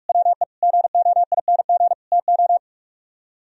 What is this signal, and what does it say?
Morse code, heard as a keyed tone, about 48 wpm, 700 Hz. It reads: WE GOING TO